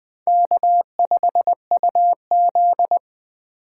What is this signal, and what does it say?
Morse code, heard as a keyed tone, about 21 wpm, 700 Hz. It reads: K5UZ
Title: USA Call Signs - 4 Characters - 4x Character Spacing - 20wpm